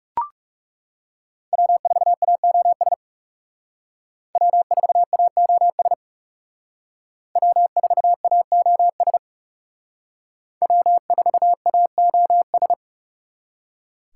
Morse code, heard as a keyed tone, about 38 wpm, 700 Hz. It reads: W4AOS W4AOS W4AOS W4AOS